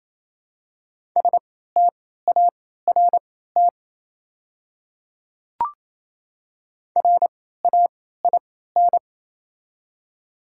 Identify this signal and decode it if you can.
Morse code, heard as a keyed tone, about 27 wpm, spaced as for 16 wpm, 700 Hz. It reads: START RAIN
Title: 3x Character Spacing - Top 300 Words - 28wpm